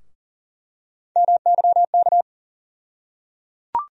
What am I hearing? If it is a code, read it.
Morse code, heard as a keyed tone, about 37 wpm, 700 Hz. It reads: MYK
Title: USA Call Sign Postfixes - 3 Letters 40wpm